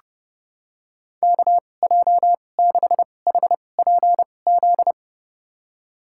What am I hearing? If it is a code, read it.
Morse code, heard as a keyed tone, about 30 wpm, spaced as for 22 wpm, 700 Hz. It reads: KJ6HPZ
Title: USA Call Signs - 6 Characters - 2x Character Spacing - 30wpm